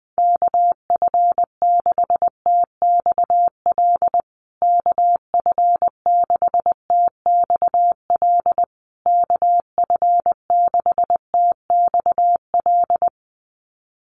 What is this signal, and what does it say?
Morse code, heard as a keyed tone, about 20 wpm, 700 Hz. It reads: KF6TXL KF6TXL KF6TXL
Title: USA Call Signs Repeated 3x 20wpm